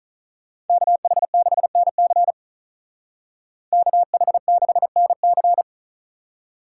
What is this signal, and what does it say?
Morse code, heard as a keyed tone, about 43 wpm, 700 Hz. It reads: KH6NC KH6NC
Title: Speed Racing - USA 2x2 Call Signs 35wpm